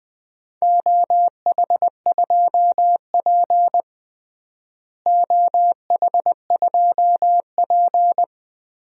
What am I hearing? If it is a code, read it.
Morse code, heard as a keyed tone, about 20 wpm, 700 Hz. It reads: OH2P OH2P
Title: DXCC-100 Call Signs - 4 Characters - Repeated 3x - 20wpm